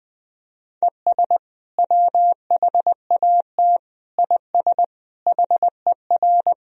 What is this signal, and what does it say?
Morse code, heard as a keyed tone, about 21 wpm, 700 Hz. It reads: ES WHAT IS HER
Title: Sentences from Top 200 Words and Abbreviations 20wpm